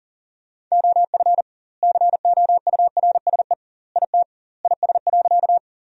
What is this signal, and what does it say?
Morse code, heard as a keyed tone, about 40 wpm, 700 Hz. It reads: OF COURSE IT IS.
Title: 2x Word-Spacing - Sentences from Top 500 Words - 40wpm